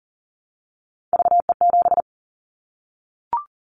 Morse code, heard as a keyed tone, about 40 wpm, 700 Hz. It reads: VE7
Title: Callsign Prefixes for Top 50 DXCC - 40wpm